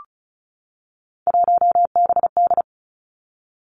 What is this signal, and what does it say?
Morse code, heard as a keyed tone, about 36 wpm, 700 Hz. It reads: 1BD
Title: Three Challenging Letter-Number Combinations 35wpm